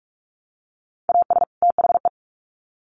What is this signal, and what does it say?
Morse code, heard as a keyed tone, about 44 wpm, 700 Hz. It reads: AS THE
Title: Sets of 2 Words 45wpm